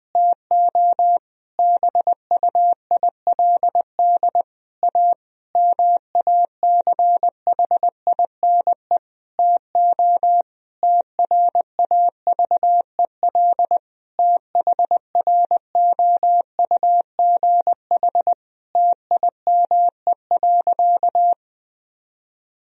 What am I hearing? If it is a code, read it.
Morse code, heard as a keyed tone, about 20 wpm, 700 Hz. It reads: TO BUILD A MACHINE TO TRAVEL THROUGH TIME.
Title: Sentences from Top 1000 Words - Review Part 2 - 20wpm